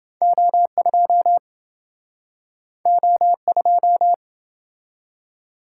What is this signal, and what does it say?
Morse code, heard as a keyed tone, about 28 wpm, 700 Hz. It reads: O2 O2